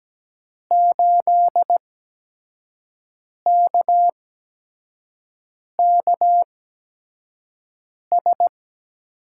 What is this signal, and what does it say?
Morse code, heard as a keyed tone, about 17 wpm, spaced as for 5 wpm, 700 Hz. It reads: 8KKS
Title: USA Call Signs - 8x Character Spacing - 17wpm